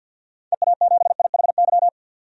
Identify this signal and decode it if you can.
Morse code, heard as a keyed tone, about 51 wpm, 700 Hz. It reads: EA7IHY
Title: DXCC-100 Call Signs - 6 Characters - 50wpm